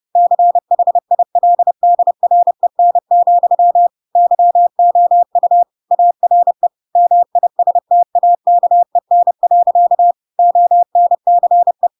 Morse code, heard as a keyed tone, about 30 wpm, 700 Hz. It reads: CHILDREN, YOU ARE MISTAKEN. ONCE
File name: Pinocchio / Pinocchio ch1 30 30 ik1hge